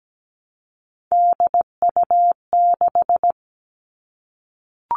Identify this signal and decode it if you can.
Morse code, heard as a keyed tone, about 17 wpm, 700 Hz. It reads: DU6
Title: All Cut Numbers - 3-Digit 17wpm